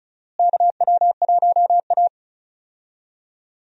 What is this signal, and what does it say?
Morse code, heard as a keyed tone, about 34 wpm, 700 Hz. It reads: KW1A
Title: USA Call Signs - 4 Characters - Repeated 2x - 35wpm